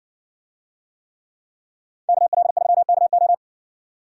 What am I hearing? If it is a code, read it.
Morse code, heard as a keyed tone, about 60 wpm, 700 Hz. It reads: KL3KO